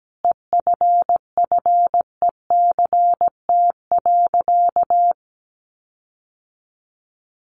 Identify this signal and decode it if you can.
Morse code, heard as a keyed tone, about 17 wpm, 700 Hz. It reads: EFFECT.
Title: Sentences from Top 1400 Words 17wpm